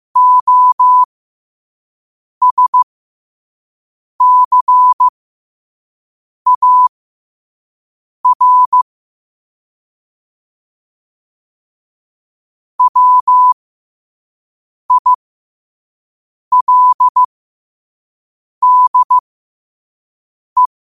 Morse code, heard as a keyed tone, about 15 wpm, spaced as for 5 wpm, 1000 Hz. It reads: OSCAR WILDE
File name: qotdfiles / 5wpm